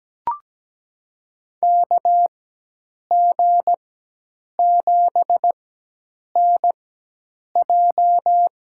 Morse code, heard as a keyed tone, about 17 wpm, spaced as for 8 wpm, 700 Hz. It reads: KG7NJ
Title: USA Call Signs - 4x Character Spacing - 17wpm